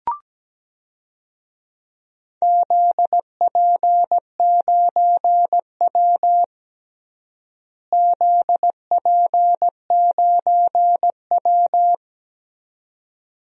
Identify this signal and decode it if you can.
Morse code, heard as a keyed tone, about 17 wpm, 700 Hz. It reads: ZP9W ZP9W
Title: DXCC-100 Call Signs - 4 Characters - Repeated 2x - 17wpm